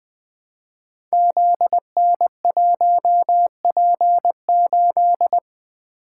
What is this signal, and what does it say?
Morse code, heard as a keyed tone, about 21 wpm, 700 Hz. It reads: ZN1P8
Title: Code Groups - 5 Letter:Numbers - 20wpm